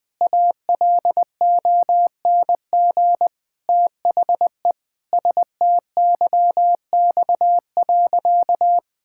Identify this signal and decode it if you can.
Morse code, heard as a keyed tone, about 21 wpm, 700 Hz. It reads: ALONG THE STYX.
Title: Pangram Challenge v2 - 20wpm